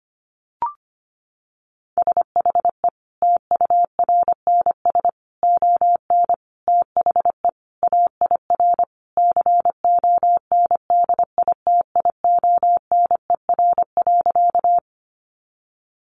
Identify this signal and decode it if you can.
Morse code, heard as a keyed tone, about 25 wpm, 700 Hz. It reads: SHE TURNS ON THE AIR CONDITIONER.
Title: Additional Easy Phrases 25wpm